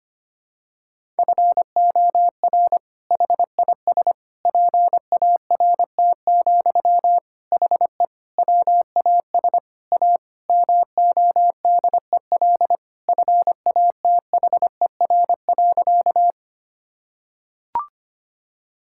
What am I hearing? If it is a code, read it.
Morse code, heard as a keyed tone, about 25 wpm, 700 Hz. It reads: FOR HIS PART, HE WAS A MODEL FATHER.
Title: Sentences from Top 1000 Words - Review Part 5 - 25wpm